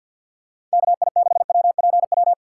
Morse code, heard as a keyed tone, about 46 wpm, 700 Hz. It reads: KI6WPW
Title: USA Call Signs - 3x Character Spacing - 50wpm